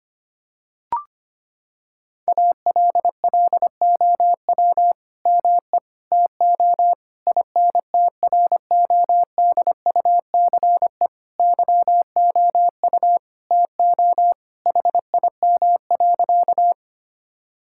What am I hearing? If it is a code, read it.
Morse code, heard as a keyed tone, about 25 wpm, 700 Hz. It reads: ALLOW ME TO INTRODUCE YOU TO HIM.
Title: Sentences from Top 1600 Words 25wpm